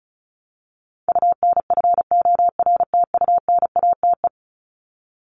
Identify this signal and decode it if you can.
Morse code, heard as a keyed tone, about 34 wpm, 700 Hz. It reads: UNFORTUNATE
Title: LY Suffix - Top 100 Words with Base Words 35wpm